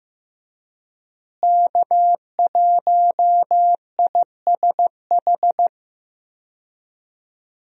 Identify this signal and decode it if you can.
Morse code, heard as a keyed tone, about 15 wpm, 700 Hz. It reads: K1ISH
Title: USA Call Signs - 5 Characters - Repeated 3x - 15wpm